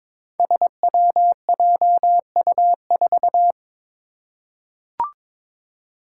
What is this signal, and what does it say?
Morse code, heard as a keyed tone, about 23 wpm, 700 Hz. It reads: SWJU4
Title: Five Challenging Letter-Number Combinations 22wpm